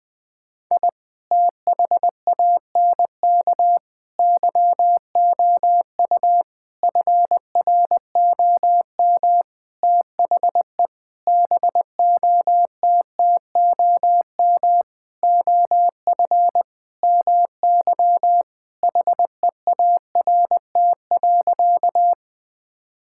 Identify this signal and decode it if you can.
Morse code, heard as a keyed tone, about 20 wpm, 700 Hz. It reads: I THANK YOU FROM THE BOTTOM OF MY HEART.
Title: Sentences with 6-Word Phrases - 20wpm